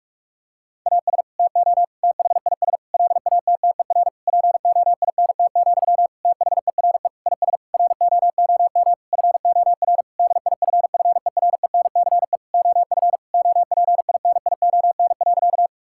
Morse code, heard as a keyed tone, about 46 wpm, 700 Hz. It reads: AS TO THIS LATTER POINT, THERE IS ROOM FOR DIFFERENCE OF OPINION.
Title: Sentences from Top 1600 Words 45wpm